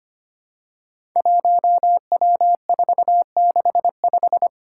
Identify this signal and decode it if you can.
Morse code, heard as a keyed tone, about 25 wpm, 700 Hz. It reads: 1W465
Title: Five Challenging Letter-Number Combinations 25wpm